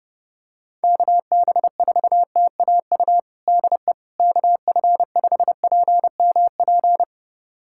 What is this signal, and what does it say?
Morse code, heard as a keyed tone, about 32 wpm, 700 Hz. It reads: KB4TAU DE KF5PMP
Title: USA Callsign DE Callsign - 6 Characters - 30wpm